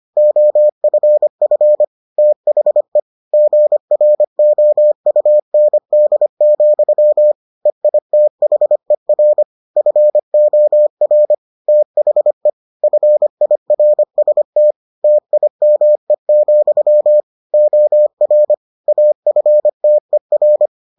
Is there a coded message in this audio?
Yes: Morse code from 0.2 to 20.7 s, heard as a keyed tone, about 25 wpm, 600 Hz. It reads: OFF THE GROUND, EITHER FOR THE FIRST TIME, OR AFTER